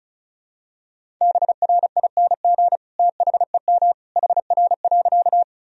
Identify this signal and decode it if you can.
Morse code, heard as a keyed tone, about 36 wpm, 700 Hz. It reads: BRING THEM HR.
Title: Sentences from Top 500 Words and Abbreviations 35wpm